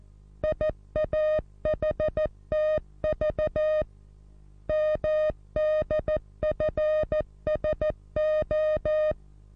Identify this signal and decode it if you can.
Morse code, heard as a keyed tone, about 14 wpm, 620 Hz. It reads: IAHTV MDFSO
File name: repos / Morsekurs / DJ4UF Morsen27